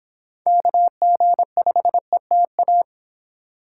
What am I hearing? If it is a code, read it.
Morse code, heard as a keyed tone, about 26 wpm, 700 Hz. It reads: KG5ETA